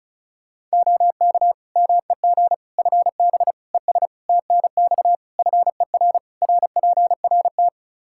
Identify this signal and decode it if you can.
Morse code, heard as a keyed tone, about 36 wpm, 700 Hz. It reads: OK MEG FB ES TNX FER RPRT
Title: Practice QSO - Exchange 2 - Info - 35wpm